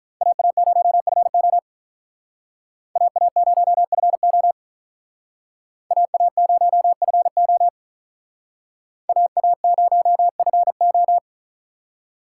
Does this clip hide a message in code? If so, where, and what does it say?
Yes: Morse code from 0.2 to 11.2 s, heard as a keyed tone, about 45 wpm, 700 Hz. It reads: AA0FO AA0FO AA0FO AA0FO